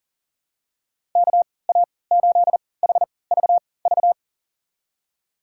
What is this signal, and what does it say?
Morse code, heard as a keyed tone, about 40 wpm, spaced as for 23 wpm, 700 Hz. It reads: KA8HVV
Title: USA Call Signs - 3x Character Spacing - 40wpm